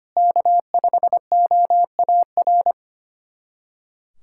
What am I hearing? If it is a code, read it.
Morse code, heard as a keyed tone, about 25 wpm, 700 Hz. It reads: K5OAR
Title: USA Callsign DE Callsign - 5 Characters - 25wpm